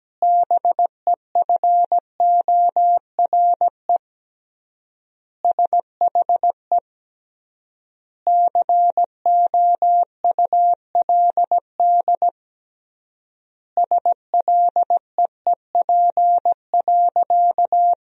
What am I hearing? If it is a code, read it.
Morse code, heard as a keyed tone, about 17 wpm, 700 Hz. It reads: BEFORE SHE COULD SLEEP.
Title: 3x Word-Spacing - Sentences from Top 500 Words - 17wpm